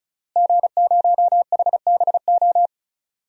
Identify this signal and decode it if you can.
Morse code, heard as a keyed tone, about 36 wpm, 700 Hz. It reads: G0HBO
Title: DXCC-100 Call Signs - 5 Characters - Repeated 3x - 35wpm